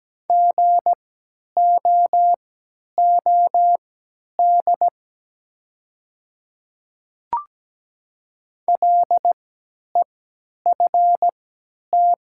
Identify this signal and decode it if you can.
Morse code, heard as a keyed tone, about 17 wpm, spaced as for 10 wpm, 700 Hz. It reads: GOOD LEFT
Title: 3x Character Spacing - Top 300 Words - 17wpm